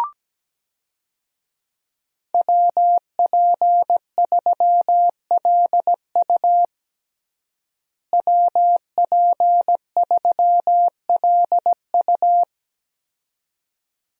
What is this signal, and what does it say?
Morse code, heard as a keyed tone, about 17 wpm, 700 Hz. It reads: WP3LU WP3LU
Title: USA Call Signs - 5 Characters - Repeated 2x - 17wpm